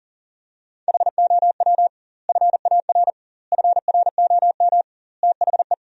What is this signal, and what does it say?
Morse code, heard as a keyed tone, about 40 wpm, 700 Hz. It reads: HOW FAR FROM THE
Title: 2x Word-Spacing - Sentences from Top 400 Words - 40wpm